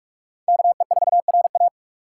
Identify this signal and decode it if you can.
Morse code, heard as a keyed tone, about 42 wpm, 700 Hz. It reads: KE4RA
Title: USA Call Signs - 5 Characters - 4x Character Spacing - 45wpm